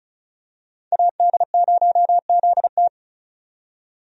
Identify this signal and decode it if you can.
Morse code, heard as a keyed tone, about 36 wpm, 700 Hz. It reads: AD0ZT
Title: USA Call Signs - 5 Characters - 35wpm